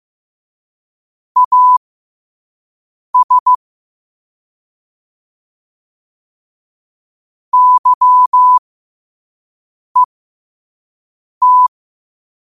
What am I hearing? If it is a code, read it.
Morse code, heard as a keyed tone, about 15 wpm, spaced as for 5 wpm, 1000 Hz. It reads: AS YET
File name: qotdfiles / 5wpm